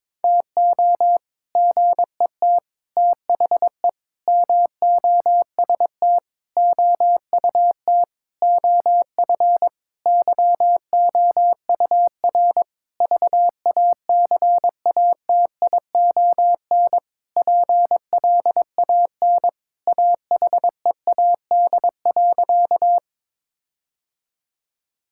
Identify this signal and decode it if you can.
Morse code, heard as a keyed tone, about 22 wpm, 700 Hz. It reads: TO GET THE MOST OUT OF YOUR VACATION PLAN AHEAD.
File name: Sentences with 7-Word Phrases - 22wpm